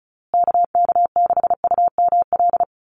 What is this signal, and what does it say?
Morse code, heard as a keyed tone, about 34 wpm, 700 Hz. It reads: KK6UML
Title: USA Call Signs Repeated 3x 35wpm